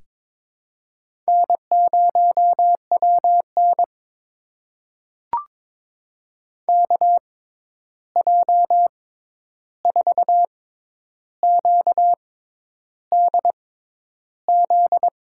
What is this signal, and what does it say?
Morse code, heard as a keyed tone, about 23 wpm, spaced as for 10 wpm, 700 Hz. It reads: N0WN KJ4QDZ
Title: USA Call Signs - 6x Character Spacing - 22wpm